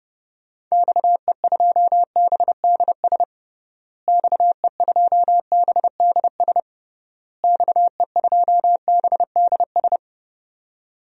Morse code, heard as a keyed tone, about 32 wpm, 700 Hz. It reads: XE2BDS XE2BDS XE2BDS
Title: DXCC-100 Call Signs - 6 Characters - Repeated 3x - 30wpm